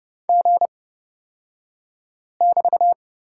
Morse code, heard as a keyed tone, about 30 wpm, spaced as for 5 wpm, 700 Hz. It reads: G<BT>
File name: Lesson 36d - BT - Two Characters 30wpm